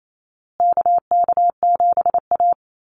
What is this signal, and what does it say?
Morse code, heard as a keyed tone, about 28 wpm, 700 Hz. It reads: KK7A
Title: Call Signs Repeated 2x - 28wpm